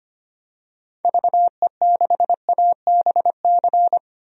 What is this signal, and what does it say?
Morse code, heard as a keyed tone, about 25 wpm, 700 Hz. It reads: VE6ABC
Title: DXCC-100 Call Signs - 6 Characters - 25wpm